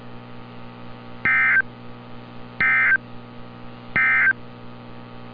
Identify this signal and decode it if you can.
Emergency Alert System (SAME) tones, heard as harsh buzzing bursts. It is heard from 1.1 to 4.4 s.
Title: NOAA Weather Radio - Altoona, PA